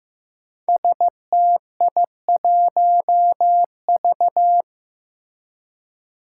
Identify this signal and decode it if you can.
Morse code, heard as a keyed tone, about 15 wpm, 700 Hz. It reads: STI1V
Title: Five Challenging Letter-Number Combinations 15wpm